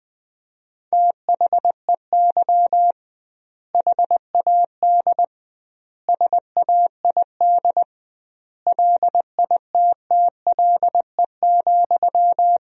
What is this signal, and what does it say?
Morse code, heard as a keyed tone, about 20 wpm, 700 Hz. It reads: THEY HAD SAID LITTLE,
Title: 2x Word-Spacing - Sentences from Top 400 Words - 20wpm